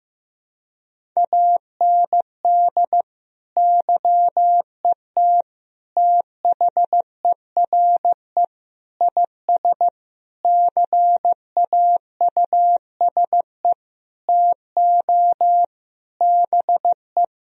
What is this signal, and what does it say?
Morse code, heard as a keyed tone, about 15 wpm, 700 Hz. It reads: AND YET THERE IS CAUSE TO BE
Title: Sentences from Top 1000 Words - Review Part 1 - 15wpm